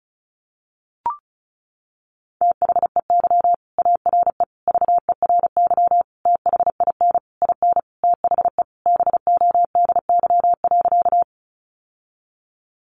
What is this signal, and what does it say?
Morse code, heard as a keyed tone, about 34 wpm, 700 Hz. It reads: THEY ARE VERY THIN IN THE BODY.
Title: Sentences from Top 1200 Words 35wpm